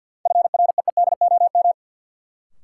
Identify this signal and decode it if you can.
Morse code, heard as a keyed tone, about 53 wpm, 700 Hz. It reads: FREEDOM
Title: Sentences from Top 800 Words - New Word Drill - 50wpm